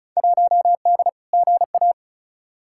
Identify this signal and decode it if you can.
Morse code, heard as a keyed tone, about 36 wpm, 700 Hz. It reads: 1D GA
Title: Field Day Exchanges - 35wpm